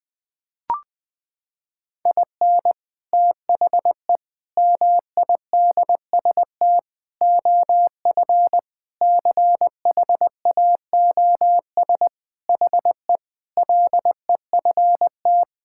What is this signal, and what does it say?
Morse code, heard as a keyed tone, about 20 wpm, 700 Hz. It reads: IN THE MIDST OF CHAOS HE LEFT
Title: Sentences with 4-Word Phrases - 20wpm